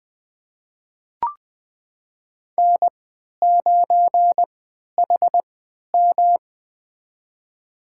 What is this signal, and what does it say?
Morse code, heard as a keyed tone, about 20 wpm, spaced as for 11 wpm, 700 Hz. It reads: N9HM
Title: USA Call Signs - 4 Characters - 3x Character Spacing - 20wpm